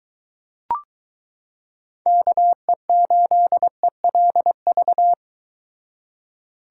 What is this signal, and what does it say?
Morse code, heard as a keyed tone, about 23 wpm, 700 Hz. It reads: KE8ELV